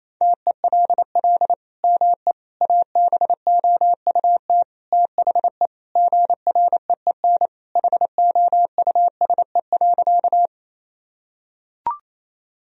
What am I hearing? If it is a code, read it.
Morse code, heard as a keyed tone, about 28 wpm, 700 Hz. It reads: TELL ME ABOUT THE GREEN HOUSE.
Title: Sentences from Top 1000 Words - Review Part 2 - 28wpm